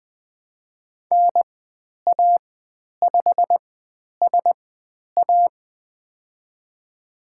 Morse code, heard as a keyed tone, about 20 wpm, spaced as for 10 wpm, 700 Hz. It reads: NA5SA
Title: 102 Lesson 17g - Letter u - Call Signs 20wpm